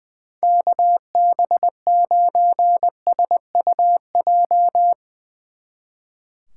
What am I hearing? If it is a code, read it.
Morse code, heard as a keyed tone, about 22 wpm, 700 Hz. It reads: KB9SUJ
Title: Speed Racing - USA 2x3 Call Signs 20wpm